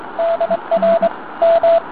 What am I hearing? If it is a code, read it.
Morse code, heard as a keyed tone, about 23 wpm, 670 Hz, with radio noise. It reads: DR M